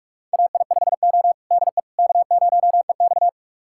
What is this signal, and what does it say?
Morse code, heard as a keyed tone, about 41 wpm, 700 Hz. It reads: AI5O DE K0EX
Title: USA Callsign DE Callsign - 4 Characters - 45wpm